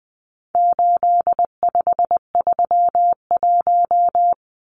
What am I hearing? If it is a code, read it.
Morse code, heard as a keyed tone, about 20 wpm, 700 Hz. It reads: 8531
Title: Numbers - 4-Digit 20wpm